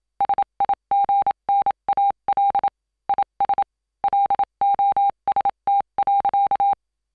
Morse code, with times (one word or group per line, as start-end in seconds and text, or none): 0.2-2.7 SIGNAL
3.1-3.6 IS
4.0-6.7 LOST.